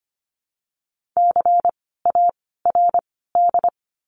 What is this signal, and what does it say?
Morse code, heard as a keyed tone, about 24 wpm, 700 Hz. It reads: C A R D
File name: course / 25:10wpm / 094 Lesson 16f - Letter c - Words 25wpm